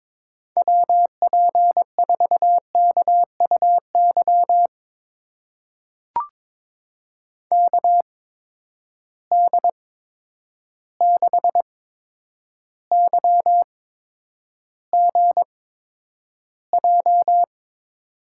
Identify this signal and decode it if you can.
Morse code, heard as a keyed tone, about 22 wpm, spaced as for 9 wpm, 700 Hz. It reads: WP4KUY KD6YGJ
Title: USA Call Signs - 8x Character Spacing - 22wpm